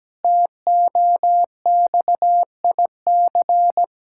Morse code, heard as a keyed tone, about 17 wpm, 700 Hz. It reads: TOXIC
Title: Pangram Challenge v2 - 17wpm